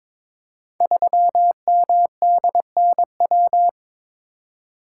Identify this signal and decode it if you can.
Morse code, heard as a keyed tone, about 22 wpm, 700 Hz. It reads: 3MDNW
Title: Code Groups - 5 Letter:Numbers - 22wpm